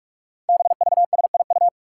Morse code, heard as a keyed tone, about 44 wpm, 700 Hz. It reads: BVSIU
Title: Five Challenging Letter-Number Combinations 45wpm